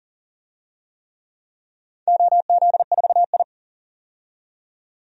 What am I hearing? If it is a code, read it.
Morse code, heard as a keyed tone, about 40 wpm, 700 Hz. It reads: OZ4I
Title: DXCC-100 Call Signs - 4 Characters - 40wpm